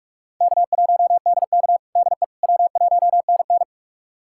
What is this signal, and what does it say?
Morse code, heard as a keyed tone, about 43 wpm, 700 Hz. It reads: K1DK DE W1NN
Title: USA Callsign DE Callsign - 4 Characters - 45wpm